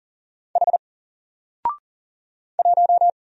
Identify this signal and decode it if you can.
Morse code, heard as a keyed tone, about 40 wpm, 700 Hz. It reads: H 1
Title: Single Character (Letter:Number:Punctuation:Prosign) 40wpm